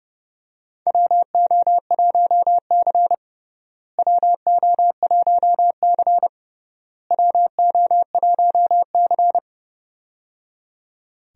Morse code, heard as a keyed tone, about 30 wpm, 700 Hz. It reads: WO1C WO1C WO1C
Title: USA Call Signs - 4 Characters - Repeated 3x - 30wpm